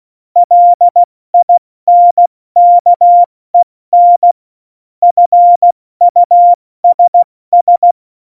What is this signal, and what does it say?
Morse code, heard as a keyed tone, about 16 wpm, 700 Hz. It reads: LINKEN FUSS
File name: Morsekurs / Witze / Witz 20. November 2018, 70 F80